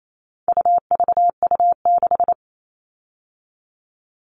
Morse code, heard as a keyed tone, about 28 wpm, 700 Hz. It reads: UVU6
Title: All Cut Numbers - 4-Digit 28wpm